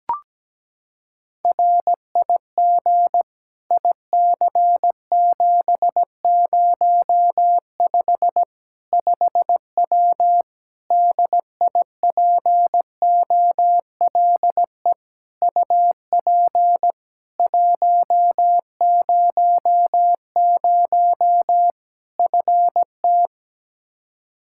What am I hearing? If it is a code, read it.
Morse code, heard as a keyed tone, about 17 wpm, 700 Hz. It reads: RIG IC705 5W DIPOLE UP 100 FT
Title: K1USN SST QSO Practice 17wpm